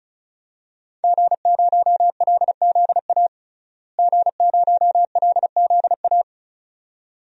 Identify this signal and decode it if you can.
Morse code, heard as a keyed tone, about 36 wpm, 700 Hz. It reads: G0LZA G0LZA
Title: DXCC-100 Call Signs - 5 Characters - Repeated 2x - 35wpm